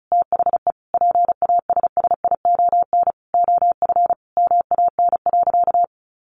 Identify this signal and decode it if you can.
Morse code, heard as a keyed tone, about 34 wpm, 700 Hz. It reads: THE PASSION OF MAN.
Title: Sentences from Top 1400 Words 35wpm